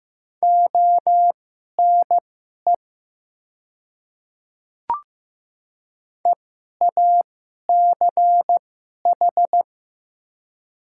Morse code, heard as a keyed tone, about 15 wpm, spaced as for 11 wpm, 700 Hz. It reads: ONE EACH
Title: 2x Character Spacing - Top 100 Words - 15wpm